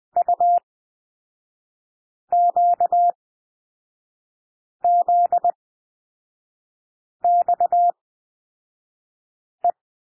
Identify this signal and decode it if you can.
Morse code, heard as a keyed tone, about 20 wpm, spaced as for 5 wpm, 700 Hz. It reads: UQZXE
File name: SOUND LETTERS 20 wpm